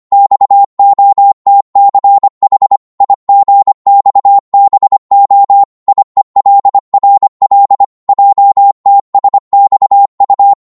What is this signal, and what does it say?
Morse code, heard as a keyed tone, about 25 wpm, 800 Hz. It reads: XOTCH IGXBO IELRL JTSXU